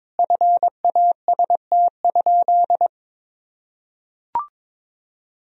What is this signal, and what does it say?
Morse code, heard as a keyed tone, about 22 wpm, 700 Hz. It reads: FAST?
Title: Sentences from Top 1000 Words - Review Part 2 - 22wpm